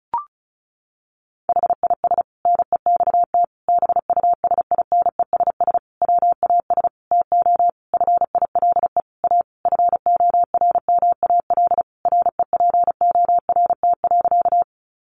Morse code, heard as a keyed tone, about 34 wpm, 700 Hz. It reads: HIS NEXT BUSINESS WAS TO FILE A FORMAL REPORT.
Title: Sentences from Top 1300 Words 35wpm